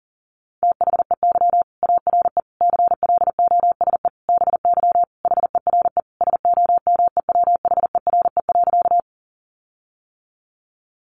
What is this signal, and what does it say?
Morse code, heard as a keyed tone, about 40 wpm, 700 Hz. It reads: THEY ARE CLOSE BY HERE SOMEWHERE.
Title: Sentences from Top 1400 Words 40wpm